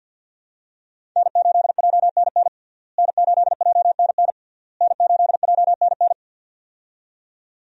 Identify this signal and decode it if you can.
Morse code, heard as a keyed tone, about 52 wpm, 700 Hz. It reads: N8JNN N8JNN N8JNN
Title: USA Call Signs - 5 Characters - Repeated 3x - 50wpm